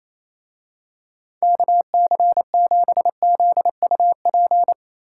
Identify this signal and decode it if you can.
Morse code, heard as a keyed tone, about 26 wpm, 700 Hz. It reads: KC7ZUP
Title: Speed Racing - USA 2x3 Call Signs 28wpm